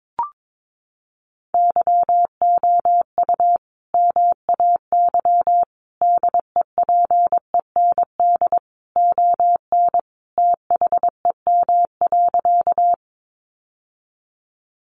Morse code, heard as a keyed tone, about 22 wpm, 700 Hz. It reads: YOU MAY DEPEND ON THEM.
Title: Sentences from Top 1300 Words 22wpm